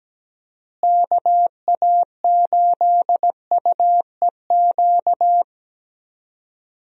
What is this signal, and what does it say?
Morse code, heard as a keyed tone, about 17 wpm, 700 Hz. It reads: KA8UEQ
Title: USA Call Signs - 6 Characters - Repeated 3x - 17wpm